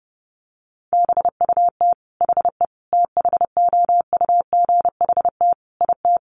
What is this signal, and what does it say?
Morse code, heard as a keyed tone, about 30 wpm, 700 Hz. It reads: BUT HE THOUGHT IT
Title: Sentences from Top 1400 Words 30wpm